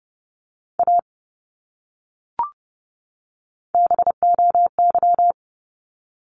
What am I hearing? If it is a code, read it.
Morse code, heard as a keyed tone, about 30 wpm, 700 Hz. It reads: A BOY
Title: Top 200 Words 30wpm